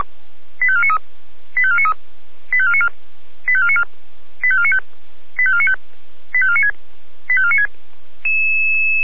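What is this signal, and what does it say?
Five-tone selective call (ZVEI), heard as quick runs of five tones, tones of about 70 ms. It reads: 85493 85493 85494 85494 85495 85495 85496 85496
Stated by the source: Five-tone ZVEI selective call received by SDR.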